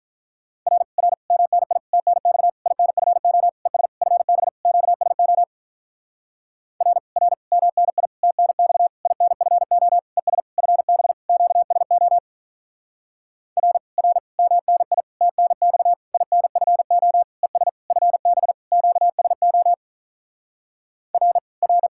Morse code, heard as a keyed tone, about 46 wpm, 700 Hz. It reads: R R MNI TNX INFO ES FB QSO R R MNI TNX INFO ES FB QSO R R MNI TNX INFO ES FB QSO R R